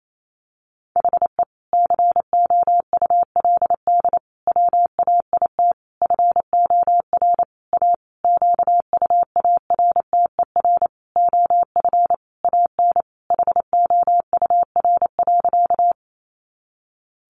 Morse code, heard as a keyed tone, about 28 wpm, 700 Hz. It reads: HE COULD WAIT FOR A QUARTER OF AN HOUR.
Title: Sentences from Top 1300 Words 28wpm